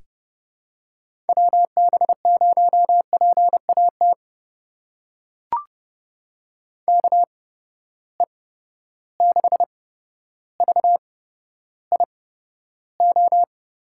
Morse code, heard as a keyed tone, about 30 wpm, spaced as for 13 wpm, 700 Hz. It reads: WB0PAT KE6VIO
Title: USA Call Signs - 8x Character Spacing - 30wpm